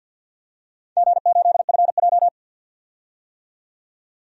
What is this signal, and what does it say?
Morse code, heard as a keyed tone, about 52 wpm, 700 Hz. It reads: G8FJ
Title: DXCC-100 Call Signs - 4 Characters - 50wpm